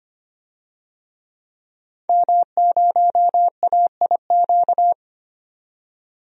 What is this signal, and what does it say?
Morse code, heard as a keyed tone, about 25 wpm, 700 Hz. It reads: M0AIQ
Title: DXCC-100 Call Signs - 5 Characters - 25wpm